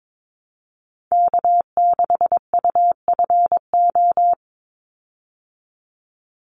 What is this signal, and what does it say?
Morse code, heard as a keyed tone, about 22 wpm, 700 Hz. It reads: K6UFO
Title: Call Signs Repeated 3x - 22wpm